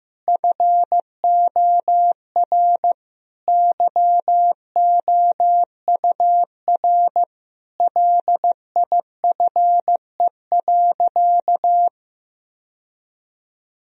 Morse code, heard as a keyed tone, about 16 wpm, 700 Hz. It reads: FOR YOUR LIFE.
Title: Sentences from Top 1000 Words - Review Part 2 - 15wpm